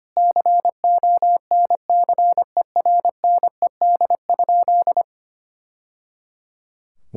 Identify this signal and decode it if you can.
Morse code, heard as a keyed tone, about 25 wpm, 700 Hz. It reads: CONCERNED?
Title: Sentences from Top 1000 Words - Review Part 6 - 25wpm